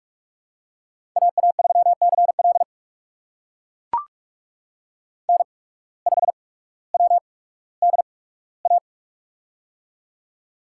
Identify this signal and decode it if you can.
Morse code, heard as a keyed tone, about 44 wpm, spaced as for 19 wpm, 700 Hz. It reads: AA3CL N5WDA
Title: USA Call Signs - 5 Characters - 8x Character Spacing - 45wpm